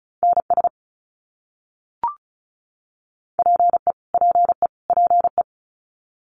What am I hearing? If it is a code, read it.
Morse code, heard as a keyed tone, about 34 wpm, 700 Hz. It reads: NS PE PE PE
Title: Canadian Provinces and Territories - Repeated 3x - 35wpm